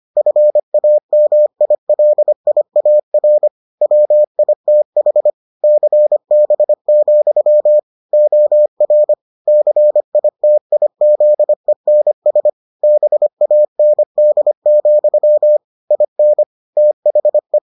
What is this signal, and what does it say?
Morse code, heard as a keyed tone, about 25 wpm, 600 Hz. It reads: FAMILIAR WITH CB, OR CITIZENS BAND, IN THE